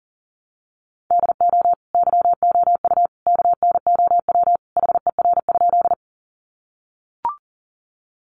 Morse code, heard as a keyed tone, about 40 wpm, 700 Hz. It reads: DO YOU KNOW HER?
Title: Sentences from Top 100 Words 40wpm